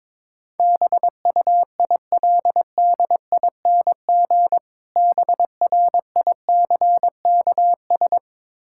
Morse code, heard as a keyed tone, about 22 wpm, 690 Hz. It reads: BUILDING BRICKS
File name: Difficult Words with Hints - 22wpm